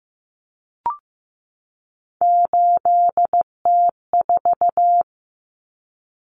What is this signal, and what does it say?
Morse code, heard as a keyed tone, about 15 wpm, 700 Hz. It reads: 8T4
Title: Common Cut Numbers - 3-Digit 15wpm